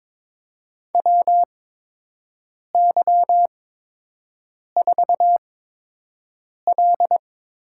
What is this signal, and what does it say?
Morse code, heard as a keyed tone, about 22 wpm, spaced as for 6 wpm, 700 Hz. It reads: WY4L
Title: USA Call Signs - 8x Character Spacing - 22wpm